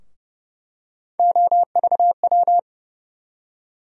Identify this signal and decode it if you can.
Morse code, heard as a keyed tone, about 30 wpm, 700 Hz. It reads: OVW
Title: USA Call Sign Postfixes - 3 Letters 30wpm